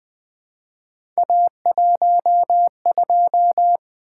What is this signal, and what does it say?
Morse code, heard as a keyed tone, about 20 wpm, 700 Hz. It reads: A12
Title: Three Letter-Number Combinations - 20wpm